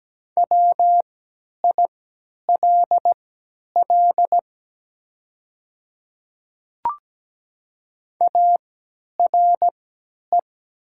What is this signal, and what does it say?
Morse code, heard as a keyed tone, about 17 wpm, spaced as for 10 wpm, 700 Hz. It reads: WILL ARE
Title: 3x Character Spacing - Top 100 Words - 17wpm